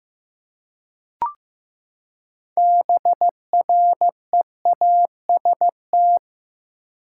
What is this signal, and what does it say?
Morse code, heard as a keyed tone, about 16 wpm, 700 Hz. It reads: BREAST
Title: Six Letter Words - 15wpm